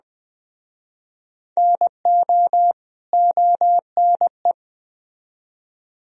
Morse code, heard as a keyed tone, about 20 wpm, 700 Hz. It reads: NO ONE